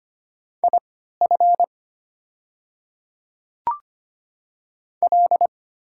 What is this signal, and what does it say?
Morse code, heard as a keyed tone, about 25 wpm, spaced as for 14 wpm, 700 Hz. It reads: IF L
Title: 3x Character Spacing - Top 200 Words - 25wpm